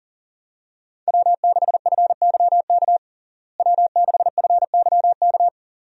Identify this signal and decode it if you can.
Morse code, heard as a keyed tone, about 37 wpm, 700 Hz. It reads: W6FYK W6FYK
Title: USA Call Signs - 5 Characters - Repeated 2x - 40wpm